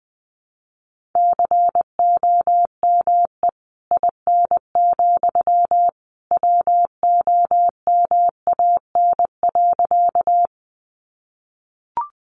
Morse code, heard as a keyed tone, about 20 wpm, 700 Hz. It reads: COME IN, WOMAN.
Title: Sentences from Top 1100 Words 20wpm